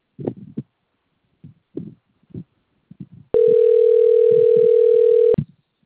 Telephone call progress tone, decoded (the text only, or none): ringback tone